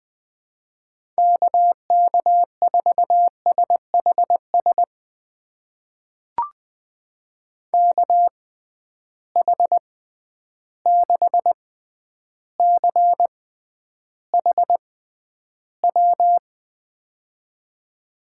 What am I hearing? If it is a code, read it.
Morse code, heard as a keyed tone, about 20 wpm, spaced as for 10 wpm, 700 Hz. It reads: KK4SHS KH6CHW
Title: USA Call Signs - 6 Characters - 6x Character Spacing - 20wpm